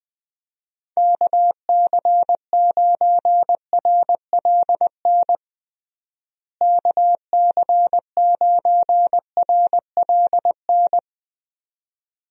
Morse code, heard as a keyed tone, about 21 wpm, 700 Hz. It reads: KC9RLN KC9RLN
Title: USA Call Signs - 6 Characters - Repeated 2x - 20wpm